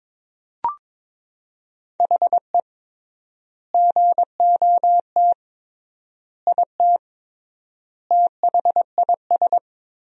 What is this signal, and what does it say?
Morse code, heard as a keyed tone, about 22 wpm, 700 Hz. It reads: HE GOT IT THIS
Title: 3x Word-Spacing - Sentences from Top 400 Words - 22wpm